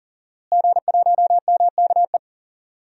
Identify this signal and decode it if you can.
Morse code, heard as a keyed tone, about 40 wpm, 700 Hz. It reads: G1MKE
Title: DXCC-100 Call Signs - 5 Characters - Repeated 3x - 40wpm